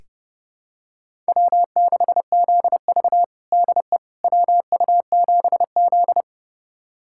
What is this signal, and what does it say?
Morse code, heard as a keyed tone, about 30 wpm, 700 Hz. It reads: W6ZV DE WU7Z
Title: USA Callsign DE Callsign - 4 Characters - 30wpm